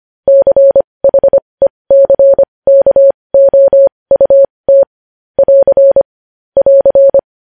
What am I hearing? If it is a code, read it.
Morse code, heard as a keyed tone, about 25 wpm, spaced as for 20 wpm, 550 Hz. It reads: CHECKOUT <AR> <AR>
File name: podcast / News headlines-1x-25x20